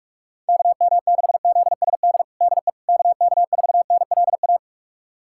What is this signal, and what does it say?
Morse code, heard as a keyed tone, about 46 wpm, 700 Hz. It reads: KM6ZSD DE KK4NLA
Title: USA Callsign DE Callsign - 6 Characters - 45wpm